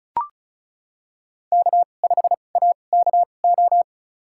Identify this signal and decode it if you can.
Morse code, heard as a keyed tone, about 34 wpm, spaced as for 25 wpm, 690 Hz. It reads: K5AKO
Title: USA Call Signs - 5 Characters - 2x Character Spacing - 35wpm